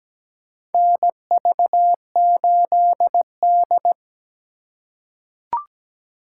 Morse code, heard as a keyed tone, about 17 wpm, 700 Hz. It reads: NV8D
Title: USA Call Signs - 4x Character Spacing - 17wpm